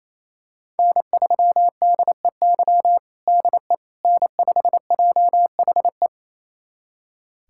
Morse code, heard as a keyed tone, about 28 wpm, 700 Hz. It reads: N3DEY DE N5JHE
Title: USA Callsign DE Callsign - 5 Characters - 28wpm